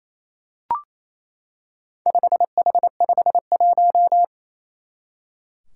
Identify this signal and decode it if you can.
Morse code, heard as a keyed tone, about 29 wpm, 700 Hz. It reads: 5H51
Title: Four Challenging Letter-Number Combinations 28wpm